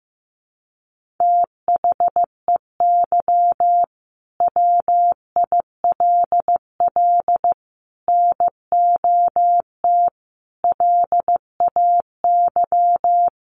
Morse code, heard as a keyed tone, about 15 wpm, 700 Hz. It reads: THEY WILL NOT LAY
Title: Sentences from Top 1500 Words 15wpm